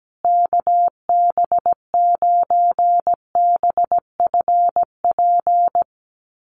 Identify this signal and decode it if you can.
Morse code, heard as a keyed tone, about 17 wpm, 700 Hz. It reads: KB9BFP
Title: USA Call Signs Repeated 3x 17wpm